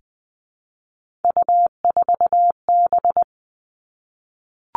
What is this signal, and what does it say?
Morse code, heard as a keyed tone, about 20 wpm, 700 Hz. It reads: U4B
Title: Three Challenging Letter-Number Combinations 20wpm